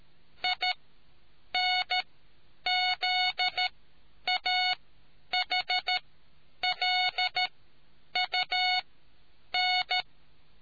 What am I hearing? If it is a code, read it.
Morse code, heard as a keyed tone, about 13 wpm, spaced as for 8 wpm, 730 Hz. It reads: INZAHLUN